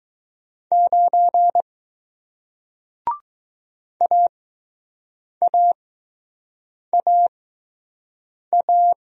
Morse code, heard as a keyed tone, about 20 wpm, spaced as for 6 wpm, 700 Hz. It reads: 9 AAAA